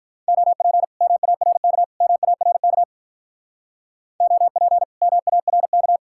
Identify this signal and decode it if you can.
Morse code, heard as a keyed tone, about 45 wpm, 700 Hz. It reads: OP MARK MARK OP MARK